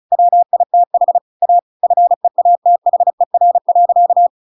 Morse code, heard as a keyed tone, about 34 wpm, 700 Hz. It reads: WITH A FEATHER.
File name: Pinocchio / Pinocchio ch8 35 35 ik1hge